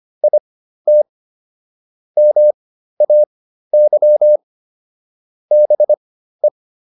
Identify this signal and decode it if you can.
Morse code, heard as a keyed tone, about 25 wpm, spaced as for 13 wpm, 600 Hz. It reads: IT MAY BE